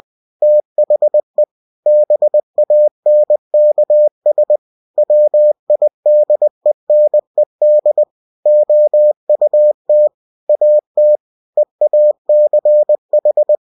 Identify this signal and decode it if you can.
Morse code, heard as a keyed tone, about 20 wpm, 600 Hz. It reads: THE BANKS WIDENED OUT AT EACH
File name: cw-books / winnie-the-pooh / winnie-20wpm-page-0083